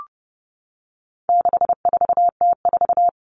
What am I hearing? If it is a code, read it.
Morse code, heard as a keyed tone, about 30 wpm, 700 Hz. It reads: 64T4
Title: All Cut Numbers - 4-Digit 30wpm